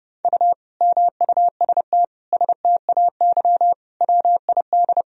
Morse code, heard as a keyed tone, about 30 wpm, 700 Hz. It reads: U MUST STAY WID
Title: Sentences from Top 500 Words and Abbreviations 30wpm